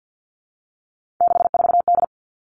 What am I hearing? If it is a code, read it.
Morse code, heard as a keyed tone, about 50 wpm, 700 Hz. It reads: 64D
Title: Three Challenging Letter-Number Combinations 50wpm